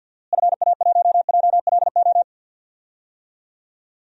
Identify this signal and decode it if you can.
Morse code, heard as a keyed tone, about 47 wpm, 700 Hz. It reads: FA1JLO
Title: DXCC-100 Call Signs - 6 Characters - Repeated 2x - 50wpm